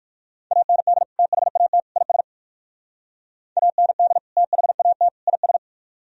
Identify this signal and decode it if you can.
Morse code, heard as a keyed tone, about 58 wpm, 700 Hz. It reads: AND THAT IS AND THAT IS